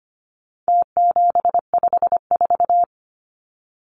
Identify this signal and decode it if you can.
Morse code, heard as a keyed tone, about 25 wpm, 700 Hz. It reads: T754
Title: Numbers - Contest Style - 4-Digit 25wpm